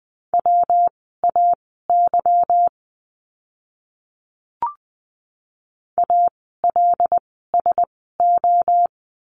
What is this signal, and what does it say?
Morse code, heard as a keyed tone, about 20 wpm, spaced as for 15 wpm, 700 Hz. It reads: WAY ALSO
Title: 244 Lesson 39f - Letter z - Words 20wpm